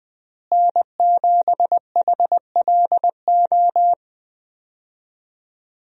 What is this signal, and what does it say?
Morse code, heard as a keyed tone, about 21 wpm, 700 Hz. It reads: N7HLO
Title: USA Call Signs - 5 Characters - Repeated 2x - 20wpm